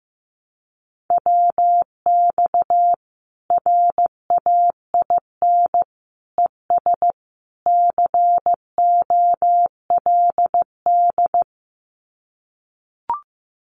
Morse code, heard as a keyed tone, about 15 wpm, 700 Hz. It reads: WX RAIN ES COLD
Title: Common QSO Phrases 15wpm